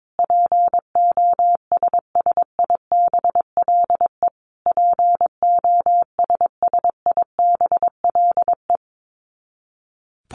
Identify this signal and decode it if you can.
Morse code, heard as a keyed tone, about 22 wpm, 700 Hz. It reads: POSSIBLE POSSIBLE
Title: Top 500 Words Repeated 22wpm